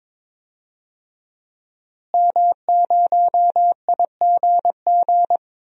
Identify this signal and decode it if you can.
Morse code, heard as a keyed tone, about 22 wpm, 700 Hz. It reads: M0IGG
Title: DXCC-100 Call Signs - 5 Characters - Repeated 2x - 22wpm